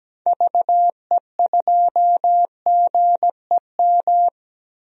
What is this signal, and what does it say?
Morse code, heard as a keyed tone, about 17 wpm, 700 Hz. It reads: VE2GEM
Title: DXCC-100 Call Signs - 6 Characters - 17wpm